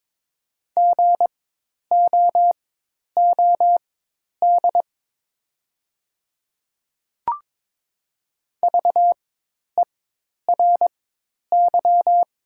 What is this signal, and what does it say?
Morse code, heard as a keyed tone, about 23 wpm, spaced as for 10 wpm, 700 Hz. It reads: GOOD VERY
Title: 4x Character Spacing - Top 500 Words - 22wpm